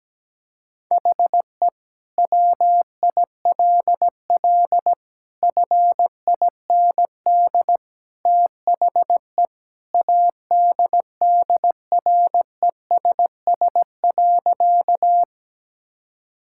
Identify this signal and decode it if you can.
Morse code, heard as a keyed tone, about 17 wpm, 700 Hz. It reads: HE WILL FIND THE ADDRESS.
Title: Sentences from Top 1000 Words - Review Part 6 - 17wpm